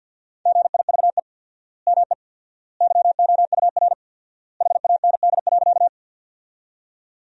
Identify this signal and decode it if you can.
Morse code, heard as a keyed tone, about 52 wpm, 700 Hz. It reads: GIVE ME YOUR HAND.
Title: 4x Word-Spacing - Sentences from Top 200 Words - 50wpm